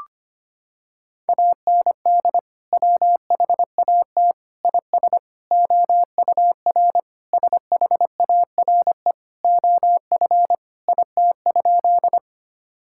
Morse code, heard as a keyed tone, about 25 wpm, 700 Hz. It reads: AND WHAT IS OUR SHARE OF IT?
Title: Sentences from Top 1000 Words - Review Part 6 - 25wpm